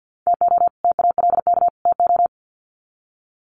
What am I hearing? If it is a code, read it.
Morse code, heard as a keyed tone, about 50 wpm, 700 Hz. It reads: TO TALK TO
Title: Sets of 3 Words 50wpm